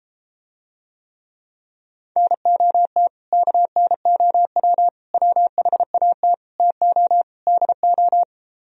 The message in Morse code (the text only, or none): NOT KNOW WHAT TO DO